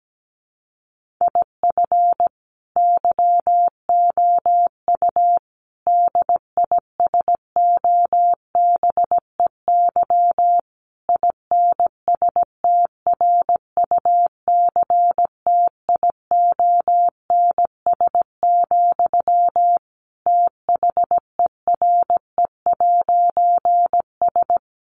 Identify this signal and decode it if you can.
Morse code, heard as a keyed tone, about 17 wpm, 700 Hz. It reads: IF YOU DISOBEY INSTRUCTIONS, THERE'S
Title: A Sound Of Thunder 17wpm